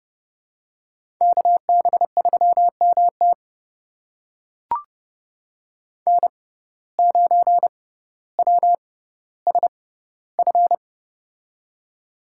Morse code, heard as a keyed tone, about 30 wpm, spaced as for 15 wpm, 700 Hz. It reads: KB3MT N9WSF
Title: USA Call Signs - 5 Characters - 6x Character Spacing - 30wpm